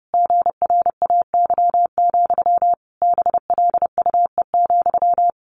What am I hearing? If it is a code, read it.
Morse code, heard as a keyed tone, about 30 wpm, 700 Hz. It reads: GRAY, BLUE,
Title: A Sound Of Thunder 30wpm